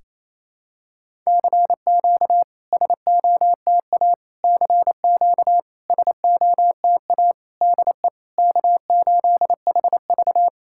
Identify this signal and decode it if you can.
Morse code, heard as a keyed tone, about 28 wpm, 700 Hz. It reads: CQ SOTA CQ SOTA DE K8HV
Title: SOTA QSO Practice 28wpm